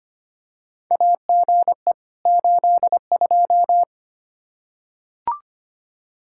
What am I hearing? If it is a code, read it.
Morse code, heard as a keyed tone, about 25 wpm, 700 Hz. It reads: AGE 82
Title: K1USN SST QSO Practice 25wpm